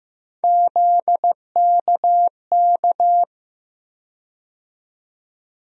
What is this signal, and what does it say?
Morse code, heard as a keyed tone, about 15 wpm, 700 Hz. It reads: ZKK
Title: USA Call Sign Postfixes - 3 Letters 15wpm